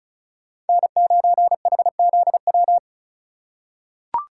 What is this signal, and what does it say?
Morse code, heard as a keyed tone, about 36 wpm, 700 Hz. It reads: N9HZW
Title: USA Call Signs - 5 Characters - 2x Character Spacing - 35wpm